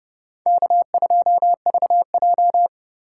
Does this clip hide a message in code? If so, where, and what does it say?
Yes: Morse code from 0.5 to 2.7 s, heard as a keyed tone, about 33 wpm, 700 Hz. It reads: K2VJ